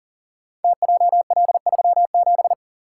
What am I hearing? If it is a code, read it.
Morse code, heard as a keyed tone, about 39 wpm, 700 Hz. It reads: TJL37
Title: Code Groups - 5 Letter:Numbers - 40wpm